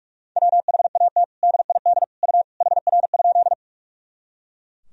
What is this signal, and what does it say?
Morse code, heard as a keyed tone, about 41 wpm, 700 Hz. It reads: WHAT DID U HR?
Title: Sentences from Top 300 Words and Abbreviations 45wpm